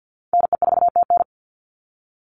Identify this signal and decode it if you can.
Morse code, heard as a keyed tone, about 50 wpm, 700 Hz. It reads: NE4TN
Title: NA SOTA Chaser Callsigns 50wpm